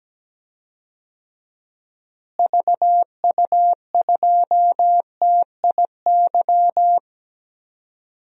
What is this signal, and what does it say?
Morse code, heard as a keyed tone, about 17 wpm, 700 Hz. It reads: VU2TIY
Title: DXCC-100 Call Signs - 6 Characters - Repeated 3x - 17wpm